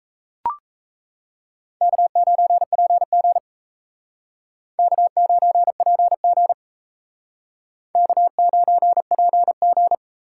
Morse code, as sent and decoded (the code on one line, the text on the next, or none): -.- ----. .--. --. / -.- ----. .--. --. / -.- ----. .--. --.
K9PG K9PG K9PG